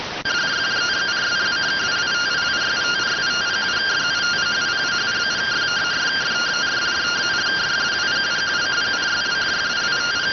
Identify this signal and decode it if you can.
Two-tone teleprinter signal, about 170 Hz shift.